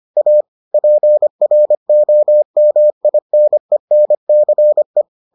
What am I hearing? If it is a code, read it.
Morse code, heard as a keyed tone, about 25 wpm, 600 Hz. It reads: A PROMINENCE